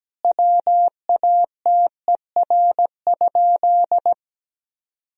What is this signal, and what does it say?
Morse code, heard as a keyed tone, about 17 wpm, 700 Hz. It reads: WATER?
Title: Sentences from Top 1000 Words - Review Part 2 - 17wpm